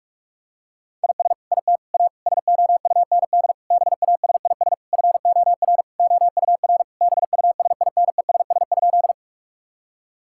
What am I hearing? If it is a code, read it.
Morse code, heard as a keyed tone, about 41 wpm, 700 Hz. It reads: IS IT A SOUND BASIS FOR OUR BUSINESS?
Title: Sentences from Top 1000 Words - Review Part 5 - 45wpm